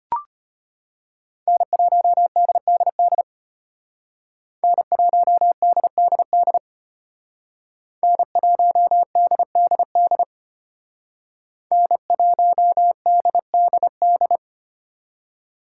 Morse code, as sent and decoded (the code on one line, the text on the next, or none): -. .---- -.. -.. -.. / -. .---- -.. -.. -.. / -. .---- -.. -.. -.. / -. .---- -.. -.. -..
N1DDD N1DDD N1DDD N1DDD